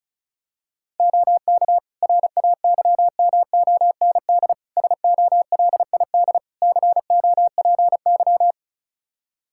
Morse code, heard as a keyed tone, about 36 wpm, 700 Hz. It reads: OK RAYMOND SOLID COPY
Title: Practice QSO - Exchange 3 - More Info - 35wpm